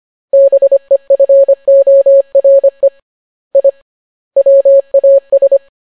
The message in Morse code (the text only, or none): BEFORE I WAS